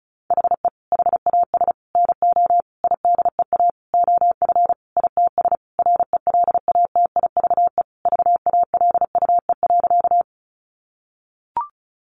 Morse code, heard as a keyed tone, about 34 wpm, 700 Hz. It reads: HE HAS NO IDEA OF ITS RELATIVE VALUE.
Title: Sentences from Top 1400 Words 35wpm